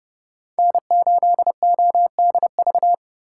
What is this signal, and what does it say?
Morse code, heard as a keyed tone, about 30 wpm, 700 Hz. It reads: N8ODV
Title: USA Call Signs - 5 Characters - 3x Character Spacing - 30wpm